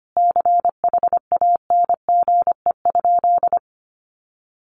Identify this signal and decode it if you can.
Morse code, heard as a keyed tone, about 25 wpm, 700 Hz. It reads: CHANGE?
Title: Sentences from Top 1500 Words 25wpm